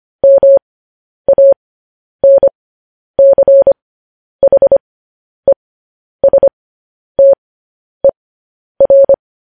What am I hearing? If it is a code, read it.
Morse code, heard as a keyed tone, about 25 wpm, spaced as for 10 wpm, 550 Hz. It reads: MANCHESTER
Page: News Headlines in Morse Code at 10 WPM